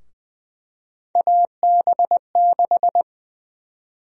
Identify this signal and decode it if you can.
Morse code, heard as a keyed tone, about 21 wpm, 700 Hz. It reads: AB6
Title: USA Call Sign Prefixes - Repeated 2x - 20wpm